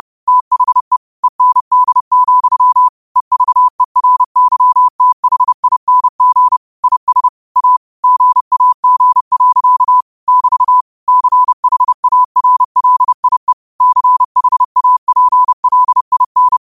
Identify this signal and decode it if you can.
Morse code, heard as a keyed tone, about 30 wpm, 1000 Hz. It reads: THE END, EVERYTHING IS A GAG. <BT> CHARLIE CHAPLIN